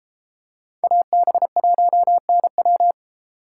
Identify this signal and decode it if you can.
Morse code, heard as a keyed tone, about 31 wpm, 700 Hz. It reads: AB1NW